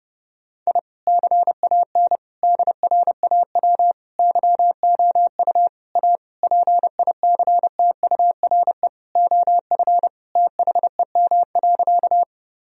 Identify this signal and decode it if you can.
Morse code, heard as a keyed tone, about 30 wpm, 700 Hz. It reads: I CAN DRAW YOU A PICTURE OF THEM.
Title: Sentences from Top 1000 Words - Review Part 1 - 30wpm